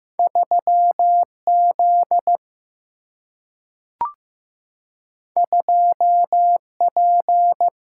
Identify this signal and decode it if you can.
Morse code, heard as a keyed tone, about 15 wpm, 700 Hz. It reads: 3 Z 2 P
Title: Single-Letter Single-Number 15wpm